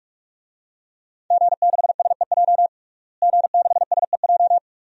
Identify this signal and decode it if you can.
Morse code, heard as a keyed tone, about 46 wpm, 700 Hz. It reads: G6SEJ G6SEJ
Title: DXCC-100 Call Signs - 5 Characters - Repeated 2x - 45wpm